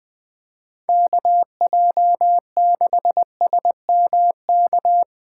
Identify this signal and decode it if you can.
Morse code, heard as a keyed tone, about 23 wpm, 700 Hz. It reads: KJ6SMK
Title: Speed Racing - USA 2x3 Call Signs 20wpm